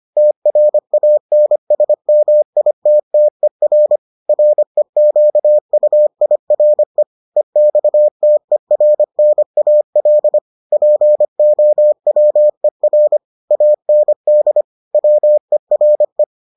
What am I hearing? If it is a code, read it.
Morse code, heard as a keyed tone, about 25 wpm, 600 Hz. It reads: TRANSMITTER REQUIRE EXTERNAL POWER AND WERE